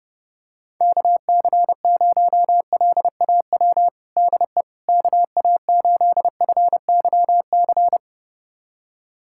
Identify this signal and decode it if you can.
Morse code, heard as a keyed tone, about 32 wpm, 700 Hz. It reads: KC0LAW DE KA8FYC
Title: USA Callsign DE Callsign - 6 Characters - 30wpm